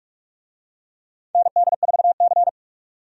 Morse code, heard as a keyed tone, about 43 wpm, 700 Hz. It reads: ND4C
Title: USA Call Signs - 4 Characters - 45wpm